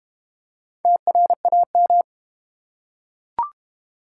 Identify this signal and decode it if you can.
Morse code, heard as a keyed tone, about 32 wpm, 700 Hz. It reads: TRAM